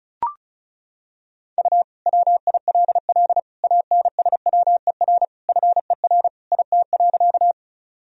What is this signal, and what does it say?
Morse code, heard as a keyed tone, about 34 wpm, 700 Hz. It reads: U WILL ANSWER FER IT.
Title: Sentences from Top 300 Words and Abbreviations 35wpm